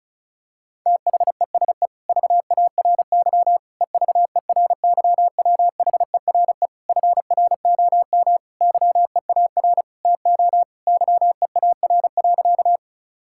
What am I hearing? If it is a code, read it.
Morse code, heard as a keyed tone, about 36 wpm, 700 Hz. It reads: THESE VARY EVERYWHERE FROM YEAR TO YEAR.
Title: Sentences from Top 1600 Words 35wpm